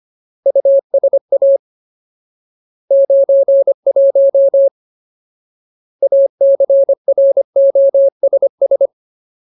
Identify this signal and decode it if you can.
Morse code, heard as a keyed tone, about 25 wpm, 550 Hz. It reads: USA 91 ACROSS
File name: Ultimate Word List / 25WPM Ultimate